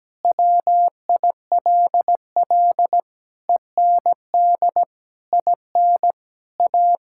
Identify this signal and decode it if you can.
Morse code, heard as a keyed tone, about 17 wpm, 700 Hz. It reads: WILL END IN A
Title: Sentences from Top 1000 Words - Review Part 5 - 17wpm